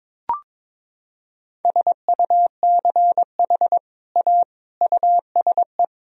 Morse code, heard as a keyed tone, about 22 wpm, 700 Hz. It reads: SUCH A USE
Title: Sentences from Top 1000 Words - Review Part 3 - 22wpm